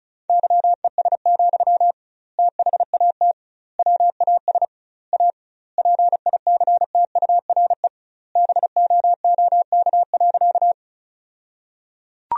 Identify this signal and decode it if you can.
Morse code, heard as a keyed tone, about 34 wpm, 700 Hz. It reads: YES, THAT WAS A PICTURE BOOK.
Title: 2x Word-Spacing - Sentences from Top 500 Words - 35wpm